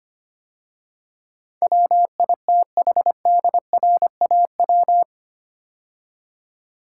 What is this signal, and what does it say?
Morse code, heard as a keyed tone, about 25 wpm, 700 Hz. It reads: WITHDRAW
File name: 8 Letter Words - First 3 Letters Given - 25wpm